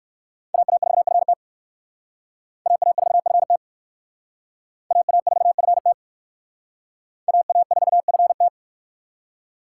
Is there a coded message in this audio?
Yes: Morse code from 0.5 to 8.5 s, heard as a keyed tone, about 59 wpm, 700 Hz. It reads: AA4FT AA4FT AA4FT AA4FT